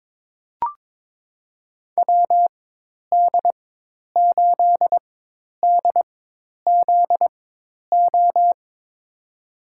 Morse code, heard as a keyed tone, about 21 wpm, spaced as for 10 wpm, 700 Hz. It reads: WD8DZO
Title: USA Call Signs - 6 Characters - 4x Character Spacing - 22wpm